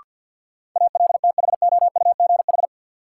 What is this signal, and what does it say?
Morse code, heard as a keyed tone, about 52 wpm, 700 Hz. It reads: ALTHOUGH
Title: Sentences from Top 600 Words - New Word Drill - 50wpm